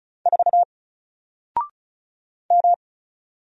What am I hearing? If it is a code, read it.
Morse code, heard as a keyed tone, about 36 wpm, 700 Hz. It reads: 4 M
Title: Single Character (Letter:Number:Punctuation:Prosign) 35wpm